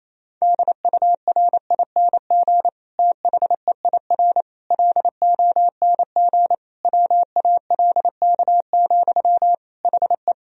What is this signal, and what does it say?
Morse code, heard as a keyed tone, about 28 wpm, 700 Hz. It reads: DURING THEIR LONG WALK, HE
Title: Sentences from Top 1000 Words - Review Part 2 - 28wpm